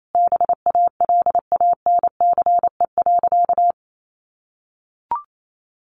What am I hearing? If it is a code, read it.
Morse code, heard as a keyed tone, about 28 wpm, 700 Hz. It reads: BALANCE.
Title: Sentences from Top 900 Words 28wpm